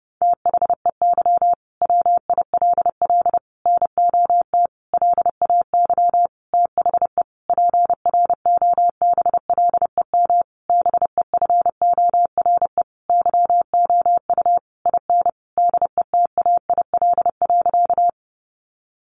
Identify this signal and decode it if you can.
Morse code, heard as a keyed tone, about 30 wpm, 700 Hz. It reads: THEY WILL NOT LAY THE PROBLEM BEFORE YOU IN DETAIL.
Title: Sentences from Top 1500 Words 30wpm